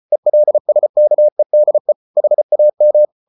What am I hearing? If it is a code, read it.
Morse code, heard as a keyed tone, about 34 wpm, 600 Hz. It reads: ELSKEDE HAM
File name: Ferie QRQ 34wpm